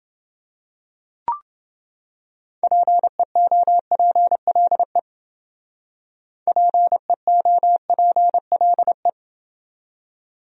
Morse code, heard as a keyed tone, about 28 wpm, 700 Hz. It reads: PEOPLE PEOPLE